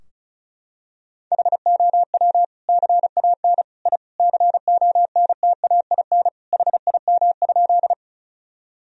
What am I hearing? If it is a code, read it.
Morse code, heard as a keyed tone, about 34 wpm, 700 Hz. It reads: HOW CAN I CONTAIN HIM?
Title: Sentences from Top 1000 Words - Review Part 2 - 35wpm